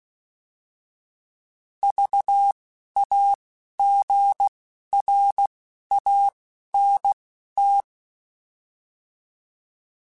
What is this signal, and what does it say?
Morse code, heard as a keyed tone, about 16 wpm, spaced as for 11 wpm, 800 Hz. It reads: VAGRANT